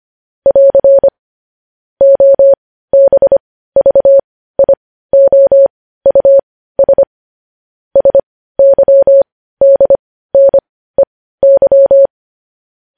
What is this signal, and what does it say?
Morse code, heard as a keyed tone, about 25 wpm, spaced as for 15 wpm, 550 Hz. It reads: <AR> OBVIOUS SYDNEY